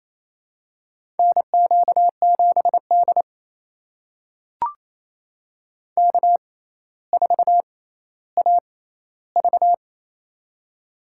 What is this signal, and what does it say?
Morse code, heard as a keyed tone, about 28 wpm, spaced as for 14 wpm, 700 Hz. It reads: NQ7D K4AV
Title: USA Call Signs - 4 Characters - 6x Character Spacing - 28wpm